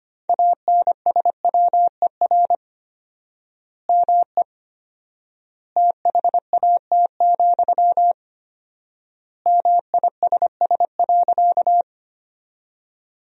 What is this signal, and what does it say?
Morse code, heard as a keyed tone, about 25 wpm, 700 Hz. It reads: ANSWER ME THAT, MISS.
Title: 4x Word-Spacing - Sentences from Top 500 Words - 25wpm